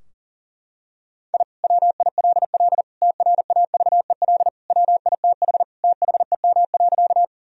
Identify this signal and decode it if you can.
Morse code, heard as a keyed tone, about 37 wpm, 700 Hz. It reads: I WILL TRAVEL WITH THEM.
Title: Sentences from Top 1000 Words - Review Part 2 - 40wpm